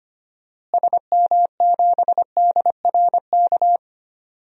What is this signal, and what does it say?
Morse code, heard as a keyed tone, about 25 wpm, 700 Hz. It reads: SM7DRK
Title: DXCC-100 Call Signs - 6 Characters - Repeated 2x - 25wpm